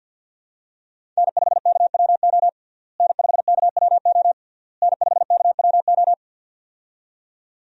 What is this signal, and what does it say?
Morse code, heard as a keyed tone, about 51 wpm, 700 Hz. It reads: N5KWO N5KWO N5KWO
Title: USA Call Signs - 5 Characters - Repeated 3x - 50wpm